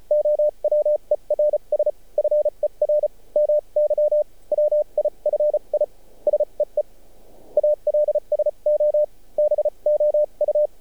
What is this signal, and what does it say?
Morse code, heard as a keyed tone, about 34 wpm, spaced as for 29 wpm, 600 Hz. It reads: OWERS FER MY WIFI SEE ALSO BOU
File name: archived / CRSnet